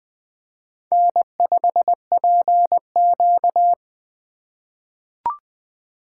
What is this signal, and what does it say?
Morse code, heard as a keyed tone, about 20 wpm, 700 Hz. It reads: N5PQ
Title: USA Call Signs - 4 Characters - 6x Character Spacing - 20wpm